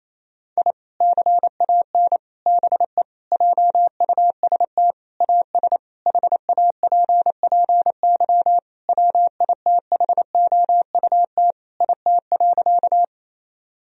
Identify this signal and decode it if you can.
Morse code, heard as a keyed tone, about 28 wpm, 700 Hz. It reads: I CAN BE JUST AS HAPPY WITHOUT IT.
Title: Sentences from Top 1000 Words - Review Part 3 - 28wpm